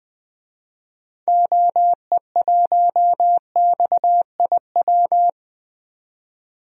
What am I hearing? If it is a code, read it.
Morse code, heard as a keyed tone, about 20 wpm, 700 Hz. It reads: OE1XIW
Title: DXCC-100 Call Signs - 6 Characters - 20wpm